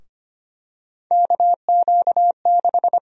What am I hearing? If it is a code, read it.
Morse code, heard as a keyed tone, about 25 wpm, 700 Hz. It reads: KQ6
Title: USA Call Sign Prefixes - Repeated 2x - 25wpm